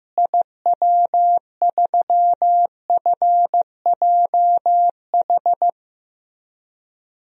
Morse code, heard as a keyed tone, about 15 wpm, 700 Hz. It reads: IW3FJH
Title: DXCC-100 Call Signs - 6 Characters - Repeated 3x - 15wpm